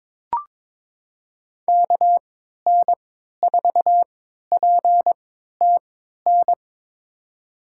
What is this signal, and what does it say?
Morse code, heard as a keyed tone, about 22 wpm, spaced as for 13 wpm, 700 Hz. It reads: KN4PTN
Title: USA Call Signs - 6 Characters - 3x Character Spacing - 22wpm